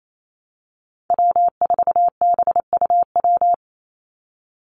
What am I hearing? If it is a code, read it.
Morse code, heard as a keyed tone, about 28 wpm, 700 Hz. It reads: W4BUW
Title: Call Signs - 28wpm